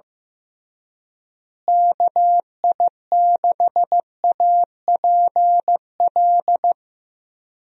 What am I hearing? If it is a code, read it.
Morse code, heard as a keyed tone, about 15 wpm, 700 Hz. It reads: KI6APL
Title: USA Call Signs - 6 Characters - Repeated 3x - 15wpm